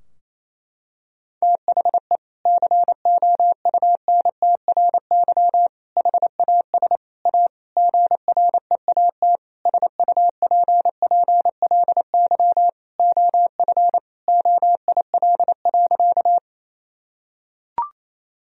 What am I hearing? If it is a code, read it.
Morse code, heard as a keyed tone, about 28 wpm, 700 Hz. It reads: THE COUNTRY HAS A GREAT SUPPLY OF OIL.
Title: Sentences from Top 1000 Words - Review Part 4 - 28wpm